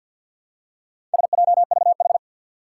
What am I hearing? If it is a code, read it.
Morse code, heard as a keyed tone, about 52 wpm, 700 Hz. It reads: SJVH
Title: Four Challenging Letter-Number Combinations 50wpm